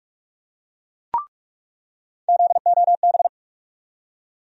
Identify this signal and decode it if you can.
Morse code, heard as a keyed tone, about 46 wpm, 700 Hz. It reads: ZOB
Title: USA Call Sign Postfixes - 3 Letters 45wpm